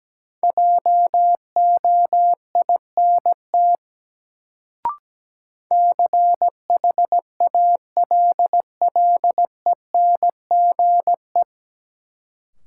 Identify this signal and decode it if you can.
Morse code, heard as a keyed tone, about 17 wpm, 700 Hz. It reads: JOINT CHALLENGE
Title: Sentences from Top 1300 Words - New Word Drill Alt - 17wpm